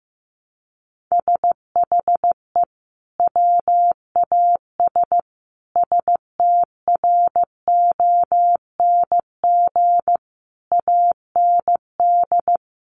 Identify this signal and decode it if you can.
Morse code, heard as a keyed tone, about 15 wpm, 700 Hz. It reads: SHE WAS STRONG AND
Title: Sentences from Top 1200 Words 15wpm